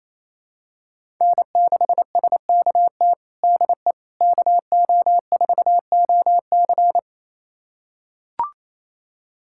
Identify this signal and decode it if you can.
Morse code, heard as a keyed tone, about 27 wpm, 700 Hz. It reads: N6SKT DE KO4OC
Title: USA Callsign DE Callsign - 5 Characters - 28wpm